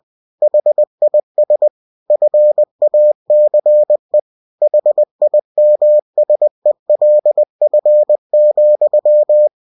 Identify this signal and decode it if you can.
Morse code, heard as a keyed tone, about 20 wpm, 600 Hz. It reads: HIS FACE HIMSELF,
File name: cw-books / winnie-the-pooh / winnie-20wpm-page-0085